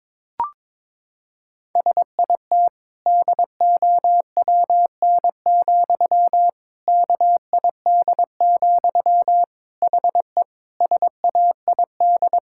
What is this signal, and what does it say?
Morse code, heard as a keyed tone, about 22 wpm, 700 Hz. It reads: SIT DOWN, KID, HE SAID
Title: Sentences from Top 1600 Words 22wpm